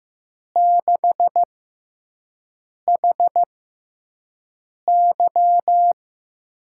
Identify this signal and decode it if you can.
Morse code, heard as a keyed tone, about 15 wpm, spaced as for 5 wpm, 700 Hz. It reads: 6HY
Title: USA Call Signs - 5 Characters - 6x Character Spacing - 15wpm